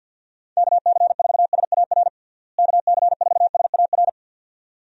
Morse code, heard as a keyed tone, about 49 wpm, 700 Hz. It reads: KC4SAR KC4SAR
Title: USA Call Signs - 6 Characters - Repeated 2x - 50wpm